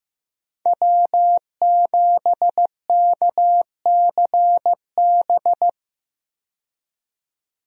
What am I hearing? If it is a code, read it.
Morse code, heard as a keyed tone, about 15 wpm, 700 Hz. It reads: W7KCB
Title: USA Call Signs - 5 Characters - 15wpm